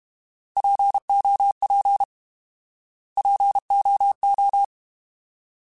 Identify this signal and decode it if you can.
Morse code, heard as a keyed tone, about 32 wpm, 800 Hz. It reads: POP POO